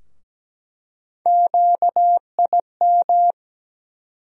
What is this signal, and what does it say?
Morse code, heard as a keyed tone, about 17 wpm, 700 Hz. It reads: QIM
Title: USA Call Sign Postfixes - 3 Letters 17wpm